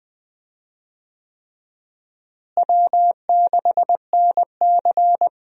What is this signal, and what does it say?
Morse code, heard as a keyed tone, about 21 wpm, 700 Hz. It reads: W6NC
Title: USA Call Signs - 4 Characters - Repeated 3x - 20wpm